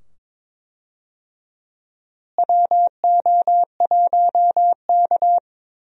Morse code, heard as a keyed tone, about 22 wpm, 700 Hz. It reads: WO1K
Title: USA Call Signs - 4 Characters - Repeated 2x - 22wpm